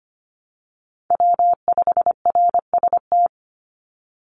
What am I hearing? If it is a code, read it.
Morse code, heard as a keyed tone, about 25 wpm, 700 Hz. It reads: W5RST
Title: NA SOTA Chaser Callsigns 25wpm